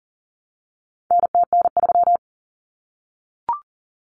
Morse code, heard as a keyed tone, about 40 wpm, 700 Hz. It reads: NTN3
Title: Common Cut Numbers - 4-Digit 40wpm